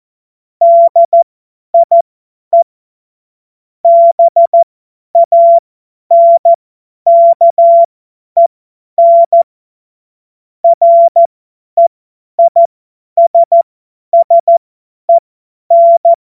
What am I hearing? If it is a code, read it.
Morse code, heard as a keyed tone, about 14 wpm, spaced as for 10 wpm, 680 Hz. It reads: DIE BANKEN REISSEN
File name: Morsekurs / Witze / Witz 10 Juni 2025 50 F70